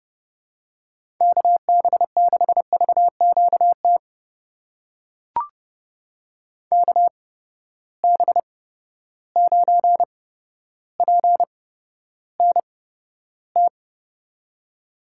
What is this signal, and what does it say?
Morse code, heard as a keyed tone, about 30 wpm, spaced as for 13 wpm, 700 Hz. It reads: KB6VQT KB9PNT
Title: USA Call Signs - 6 Characters - 8x Character Spacing - 30wpm